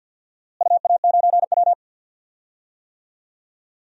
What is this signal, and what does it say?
Morse code, heard as a keyed tone, about 48 wpm, 700 Hz. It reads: UA9W
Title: DXCC-100 Call Signs - 4 Characters - Repeated 2x - 50wpm